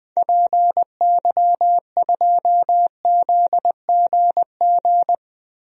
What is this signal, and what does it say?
Morse code, heard as a keyed tone, about 20 wpm, 700 Hz. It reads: PY2ZGG
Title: DXCC-100 Call Signs - 6 Characters - Repeated 2x - 20wpm